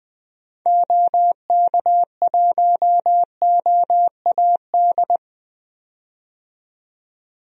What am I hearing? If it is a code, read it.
Morse code, heard as a keyed tone, about 20 wpm, 700 Hz. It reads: OK1OAD
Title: DXCC-100 Call Signs - 6 Characters - 20wpm